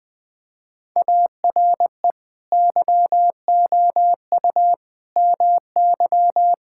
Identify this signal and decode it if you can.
Morse code, heard as a keyed tone, about 19 wpm, 700 Hz. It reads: ARE YOU MY
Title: Sentences from Top 1600 Words 20wpm